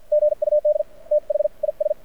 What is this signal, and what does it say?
Morse code, heard as a keyed tone, about 49 wpm, spaced as for 41 wpm, 600 Hz. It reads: GUD THIS
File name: archived / CRSnet